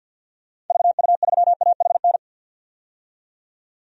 Morse code, heard as a keyed tone, about 52 wpm, 700 Hz. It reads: VU3AHN
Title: DXCC-100 Call Signs - 6 Characters - 50wpm